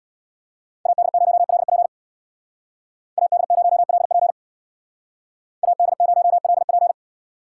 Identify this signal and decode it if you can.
Morse code, heard as a keyed tone, about 63 wpm, 700 Hz. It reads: AD0LP AD0LP AD0LP